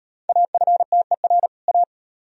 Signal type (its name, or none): Morse code